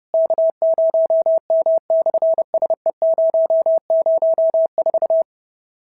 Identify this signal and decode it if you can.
Morse code, heard as a keyed tone, about 30 wpm, 650 Hz. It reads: K0M/SE004
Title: SOTA QSO Practice 30wpm